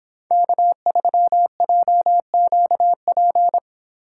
Morse code, heard as a keyed tone, about 26 wpm, 700 Hz. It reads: K3JQP